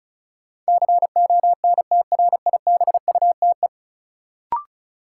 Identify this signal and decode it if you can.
Morse code, heard as a keyed tone, about 36 wpm, 700 Hz. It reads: CONTRIBUTE
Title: Sentences from Top 1400 Words - New Word Drill Alt - 35wpm